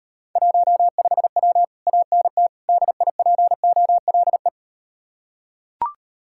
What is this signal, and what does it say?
Morse code, heard as a keyed tone, about 38 wpm, 700 Hz. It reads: 15W ANT DIPOLE